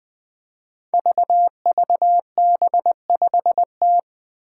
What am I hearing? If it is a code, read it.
Morse code, heard as a keyed tone, about 20 wpm, 700 Hz. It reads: VVB5T
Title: Five Challenging Letter-Number Combinations 20wpm